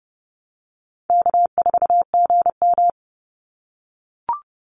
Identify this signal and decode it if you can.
Morse code, heard as a keyed tone, about 30 wpm, 700 Hz. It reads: K4GM
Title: Call Signs - 30wpm